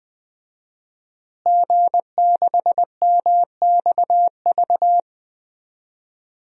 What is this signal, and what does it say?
Morse code, heard as a keyed tone, about 20 wpm, 700 Hz. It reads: G6MXV
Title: DXCC-100 Call Signs - 5 Characters - Repeated 3x - 20wpm